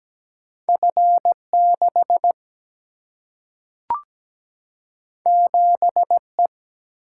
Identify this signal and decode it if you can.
Morse code, heard as a keyed tone, about 17 wpm, 700 Hz. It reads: F6 7E
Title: Single-Letter Single-Number 17wpm